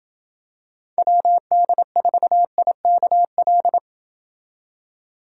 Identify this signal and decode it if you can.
Morse code, heard as a keyed tone, about 27 wpm, 700 Hz. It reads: WD4IKL